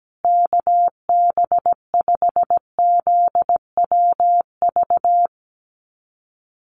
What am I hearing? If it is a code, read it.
Morse code, heard as a keyed tone, about 17 wpm, 700 Hz. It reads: KB5ZWV
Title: USA Call Signs Repeated 3x 17wpm